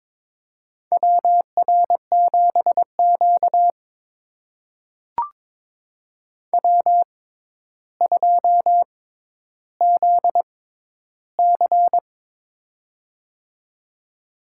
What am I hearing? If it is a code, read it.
Morse code, heard as a keyed tone, about 22 wpm, spaced as for 11 wpm, 700 Hz. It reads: WR7Q W2ZC
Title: USA Call Signs - 4 Characters - 6x Character Spacing - 22wpm